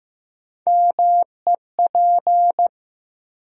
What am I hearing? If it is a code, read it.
Morse code, heard as a keyed tone, about 15 wpm, 700 Hz. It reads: MEP
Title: USA Call Sign Postfixes - 3 Letters 15wpm